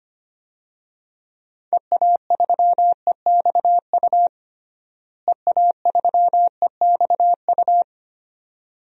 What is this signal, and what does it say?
Morse code, heard as a keyed tone, about 25 wpm, 700 Hz. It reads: EA3EXU EA3EXU
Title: DXCC-100 Call Signs - 6 Characters - Repeated 2x - 25wpm